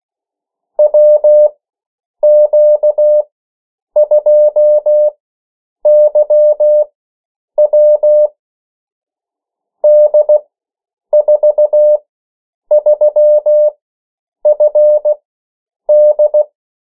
Morse code, heard as a keyed tone, about 16 wpm, spaced as for 8 wpm, 600 Hz. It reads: WQ2YW D43FD
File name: morse mp3 / 14 urok 13